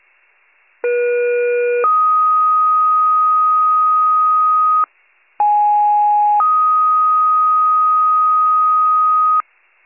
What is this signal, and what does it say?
Two-tone page, heard as a one-second tone, then a three-second tone, about 480 then 1250 Hz, about 830 then 1250 Hz. It is heard from 0.8 to 9.4 s.